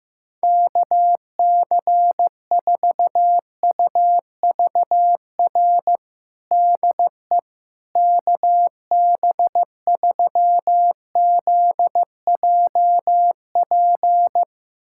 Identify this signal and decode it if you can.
Morse code, heard as a keyed tone, about 15 wpm, 700 Hz. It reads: KC4UVR DE KB3ZJP
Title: USA Callsign DE Callsign 15wpm